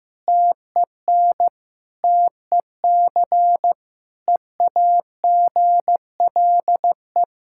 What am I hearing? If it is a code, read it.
Morse code, heard as a keyed tone, about 15 wpm, 700 Hz. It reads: TEN TEC EAGLE
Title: Practice QSO - Exchange 2 - Info - 15wpm